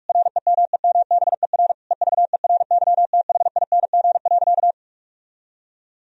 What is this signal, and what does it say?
Morse code, heard as a keyed tone, about 46 wpm, 700 Hz. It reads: REMEMBER EVERYTHING.
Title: Sentences from Top 1000 Words - Review Part 1 - 45wpm